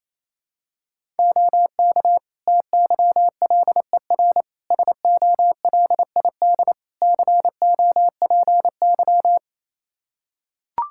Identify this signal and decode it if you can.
Morse code, heard as a keyed tone, about 28 wpm, 700 Hz. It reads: OK TYLER SOLID COPY
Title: Practice QSO - Exchange 3 - More Info - 28wpm